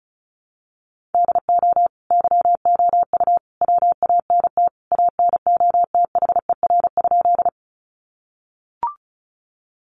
Morse code, heard as a keyed tone, about 34 wpm, 700 Hz. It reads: DO YOU WANT ANOTHER?
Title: Sentences from Top 600 Words 35wpm